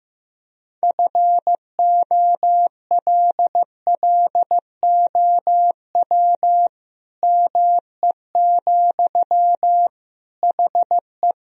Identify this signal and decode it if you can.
Morse code, heard as a keyed tone, about 15 wpm, 700 Hz. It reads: FOLLOW ME, HE
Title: Sentences from Top 1000 Words - Review Part 1 - 15wpm